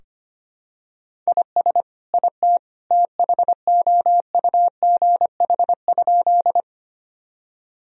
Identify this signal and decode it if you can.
Morse code, heard as a keyed tone, about 25 wpm, 700 Hz. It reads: IS IT THOUGH?
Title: Sentences from Top 1000 Words - Review Part 2 - 25wpm